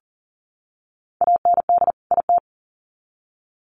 Morse code, heard as a keyed tone, about 40 wpm, 700 Hz. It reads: AND IT
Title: Sets of 2 Words 40wpm